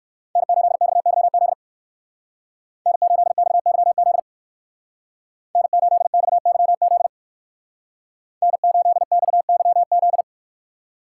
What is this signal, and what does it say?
Morse code, heard as a keyed tone, about 59 wpm, 700 Hz. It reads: N8XYZ N8XYZ N8XYZ N8XYZ